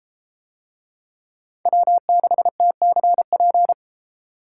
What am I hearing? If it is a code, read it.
Morse code, heard as a keyed tone, about 30 wpm, 700 Hz. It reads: W6TCP